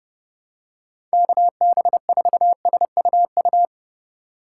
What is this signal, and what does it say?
Morse code, heard as a keyed tone, about 30 wpm, 700 Hz. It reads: KB4SUU